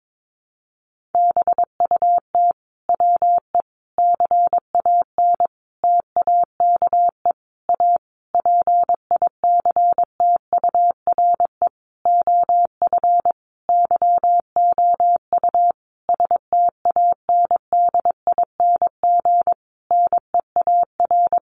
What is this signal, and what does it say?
Morse code, heard as a keyed tone, about 22 wpm, 700 Hz. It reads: BUT WE CAN TAKE A PICTURE OF YOU STANDING NEAR
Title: A Sound Of Thunder 22wpm